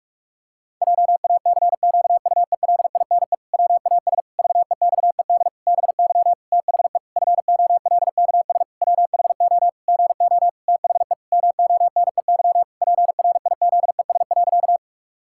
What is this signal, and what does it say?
Morse code, heard as a keyed tone, about 41 wpm, 700 Hz. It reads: JACQUELINE WAS VEXED BY THE FOLKS WHO GO THE MONEY PRIZES.
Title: Pangram Challenge v2 - 45wpm